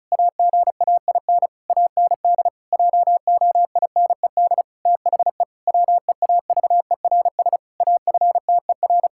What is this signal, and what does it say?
Morse code, heard as a keyed tone, about 34 wpm, 700 Hz. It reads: AGAIN AND JOINED THE WEAVERS AFTER
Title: Pangram Challenge v2 - 35wpm